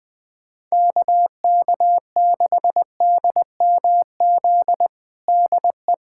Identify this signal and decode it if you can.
Morse code, heard as a keyed tone, about 21 wpm, 700 Hz. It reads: KK6DMZ DE
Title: USA Callsign DE Callsign - 6 Characters - 20wpm